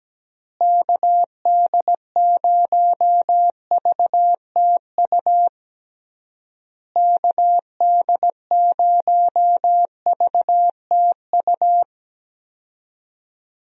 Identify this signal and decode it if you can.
Morse code, heard as a keyed tone, about 17 wpm, 700 Hz. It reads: KD0VTU KD0VTU
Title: USA Call Signs - 6 Characters - Repeated 3x - 17wpm